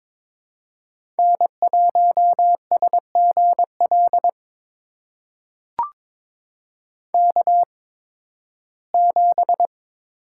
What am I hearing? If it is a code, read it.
Morse code, heard as a keyed tone, about 22 wpm, spaced as for 14 wpm, 700 Hz. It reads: N1SGL K7
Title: USA Call Signs - 5 Characters - 8x Character Spacing - 22wpm